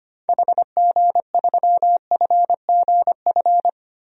Morse code, heard as a keyed tone, about 25 wpm, 700 Hz. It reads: HG3FGF
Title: DXCC-100 Call Signs - 6 Characters - Repeated 3x - 25wpm